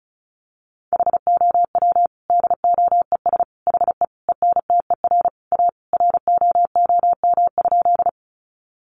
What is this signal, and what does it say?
Morse code, heard as a keyed tone, about 34 wpm, 700 Hz. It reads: HOW DOES HE ENTER A ROOM?
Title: Sentences from Top 1300 Words 35wpm